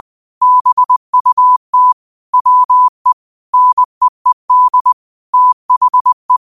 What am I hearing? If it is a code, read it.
Morse code, heard as a keyed tone, about 20 wpm, 1000 Hz. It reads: BUT WE NEED THE
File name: qotdfiles / 20wpm